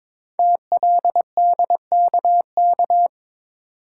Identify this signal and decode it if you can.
Morse code, heard as a keyed tone, about 23 wpm, 700 Hz. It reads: TLDKK
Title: Code Groups - 5 Letters - 22wpm